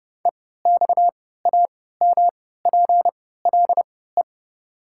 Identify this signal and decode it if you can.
Morse code, heard as a keyed tone, about 30 wpm, spaced as for 17 wpm, 700 Hz. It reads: EXAMPLE
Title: 3x Character Spacing - Top 500 Words - 30wpm